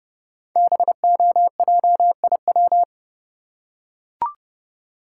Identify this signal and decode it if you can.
Morse code, heard as a keyed tone, about 30 wpm, 700 Hz. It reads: BOJIW
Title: Code Groups - 5 Letters - 30wpm